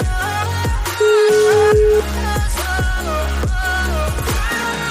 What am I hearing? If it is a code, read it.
Ringback tone (Europe, 425 Hz), with noise.